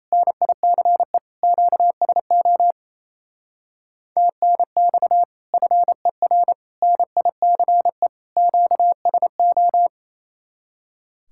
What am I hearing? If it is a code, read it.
Morse code, heard as a keyed tone, about 33 wpm, 700 Hz. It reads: NICE QSO TNX FER NICE QSO
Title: Speed-Racing Common QSO Phrases Encore - 28wpm